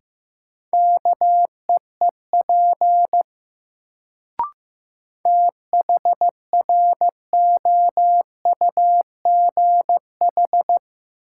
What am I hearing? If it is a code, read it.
Morse code, heard as a keyed tone, about 15 wpm, 700 Hz. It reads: KEEP THROUGH
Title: Sets of 2 Words - New Word Drill - 15wpm